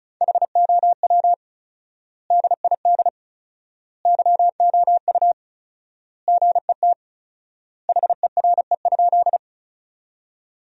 Morse code, heard as a keyed tone, about 36 wpm, 700 Hz. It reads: HOW DID YOU GET HERE?
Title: 4x Word-Spacing - Sentences from Top 200 Words - 35wpm